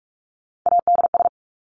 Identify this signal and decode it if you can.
Morse code, heard as a keyed tone, about 44 wpm, 700 Hz. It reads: ADS
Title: Abbreviations Repeated 3x - 45wpm